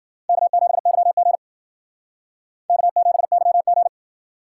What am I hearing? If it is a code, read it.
Morse code, heard as a keyed tone, about 51 wpm, 700 Hz. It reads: K7YG K7YG